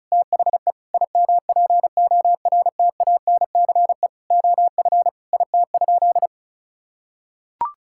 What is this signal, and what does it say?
Morse code, heard as a keyed tone, about 36 wpm, 700 Hz. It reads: THE IMPORTANCE OF IT?
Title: Sentences from Top 1600 Words 35wpm